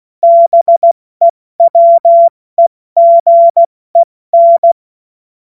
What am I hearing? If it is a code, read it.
Morse code, heard as a keyed tone, about 16 wpm, 680 Hz. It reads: BEWEGEN
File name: Morsekurs / Witze / Witz 06. Januar 2026 70 F80